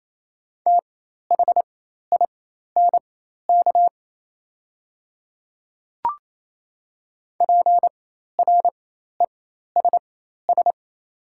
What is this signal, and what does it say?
Morse code, heard as a keyed tone, about 28 wpm, spaced as for 13 wpm, 700 Hz. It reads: THINK PRESS
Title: 4x Character Spacing - Top 500 Words - 28wpm